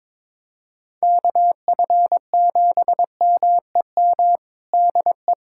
Morse code, heard as a keyed tone, about 22 wpm, 700 Hz. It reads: KF7MEM DE
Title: USA Callsign DE Callsign - 6 Characters - 22wpm